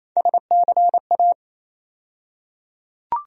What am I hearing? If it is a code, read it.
Morse code, heard as a keyed tone, about 28 wpm, 700 Hz. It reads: SCA
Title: USA Call Sign Postfixes - 3 Letters 28wpm